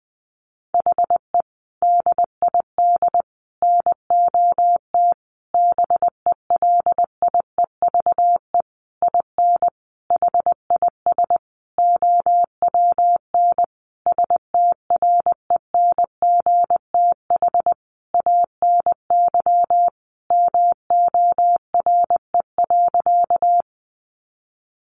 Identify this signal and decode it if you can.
Morse code, heard as a keyed tone, about 20 wpm, 700 Hz. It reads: HE DID NOT BELIEVE IN HIS OWN STRENGTH ANY MORE.
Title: Sentences from Top 1300 Words 20wpm